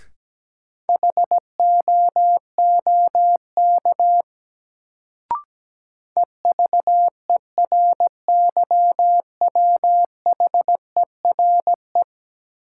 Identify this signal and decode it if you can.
Morse code, heard as a keyed tone, about 17 wpm, 700 Hz. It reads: HOOK EVERYWHERE
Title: Sentences from Top 1600 Words - New Word Drill - 17wpm